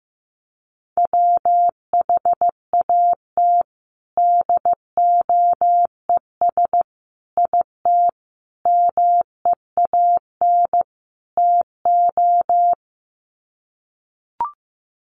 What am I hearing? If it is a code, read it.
Morse code, heard as a keyed tone, about 15 wpm, 700 Hz. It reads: WHAT DOES IT MEAN TO
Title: Sets of 5 Words Encore2 15wpm